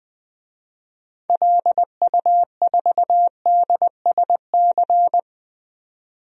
Morse code, heard as a keyed tone, about 20 wpm, 700 Hz. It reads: LU4DSC
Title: DXCC-100 Call Signs - 6 Characters - 20wpm